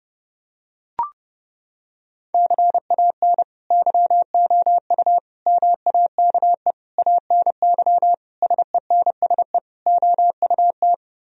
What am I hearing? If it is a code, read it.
Morse code, heard as a keyed tone, about 30 wpm, 700 Hz. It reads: CAN YOU MAKE ANY SENSE OUT
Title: Sentences from Top 1000 Words - Review Part 4 - 30wpm